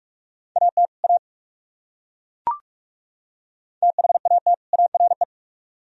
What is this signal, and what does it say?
Morse code, heard as a keyed tone, about 39 wpm, 700 Hz. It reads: AT A THAT ARE